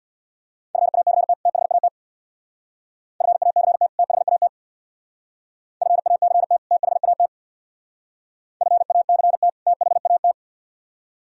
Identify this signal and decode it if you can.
Morse code, heard as a keyed tone, about 59 wpm, 700 Hz. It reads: FACT THAT FACT THAT FACT THAT FACT THAT
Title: Speed-Racing - Sets of 2 Words 50wpm